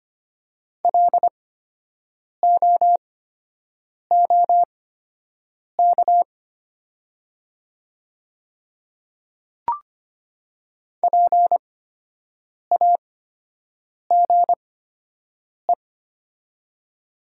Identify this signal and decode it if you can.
Morse code, heard as a keyed tone, about 25 wpm, spaced as for 7 wpm, 700 Hz. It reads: LOOK PAGE
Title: 8x Character Spacing - Top 300 Words - 25wpm